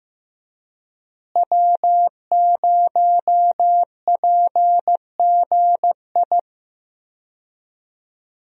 Morse code, heard as a keyed tone, about 15 wpm, 700 Hz. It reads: W0PGI
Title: USA Call Signs - 5 Characters - Repeated 3x - 15wpm